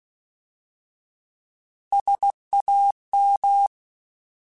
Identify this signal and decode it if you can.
Morse code, heard as a keyed tone, about 16 wpm, 800 Hz. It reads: SAM